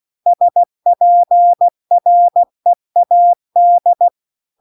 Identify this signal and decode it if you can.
Morse code, heard as a keyed tone, about 16 wpm, 700 Hz. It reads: SPREAD
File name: CW-The-Big-Twoheaded-River-2-16WPM